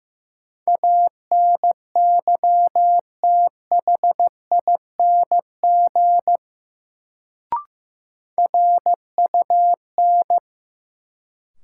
Morse code, heard as a keyed tone, about 15 wpm, 700 Hz. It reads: ANYTHING RUN
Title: Sets of 4 Words - New Word Drill - 15wpm